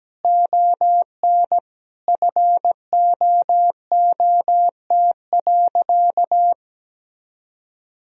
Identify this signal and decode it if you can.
Morse code, heard as a keyed tone, about 17 wpm, 700 Hz. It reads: ON FOOT.
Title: Sentences from Top 1000 Words - Review Part 3 - 17wpm